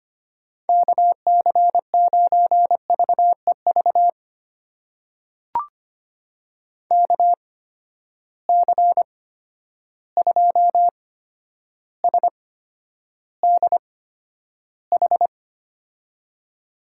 Morse code, heard as a keyed tone, about 25 wpm, spaced as for 11 wpm, 700 Hz. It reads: KC9VEV KC2SDH
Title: USA Call Signs - 6 Characters - 8x Character Spacing - 25wpm